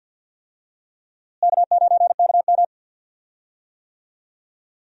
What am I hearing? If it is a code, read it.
Morse code, heard as a keyed tone, about 52 wpm, 700 Hz. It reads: K9KM
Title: USA Call Signs - 4 Characters - Repeated 3x - 50wpm